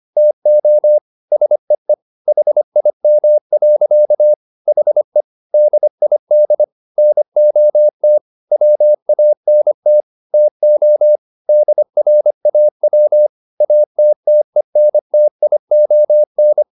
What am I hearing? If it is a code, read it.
Morse code, heard as a keyed tone, about 25 wpm, 600 Hz. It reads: TO SEE HIM. HE DID NOT WANT TO DRAW ATTENTION